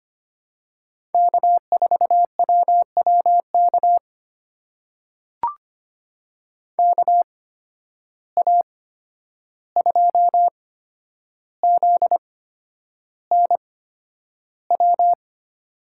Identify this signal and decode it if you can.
Morse code, heard as a keyed tone, about 25 wpm, spaced as for 10 wpm, 700 Hz. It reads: K4WWK KA2ZNW
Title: USA Call Signs - 8x Character Spacing - 25wpm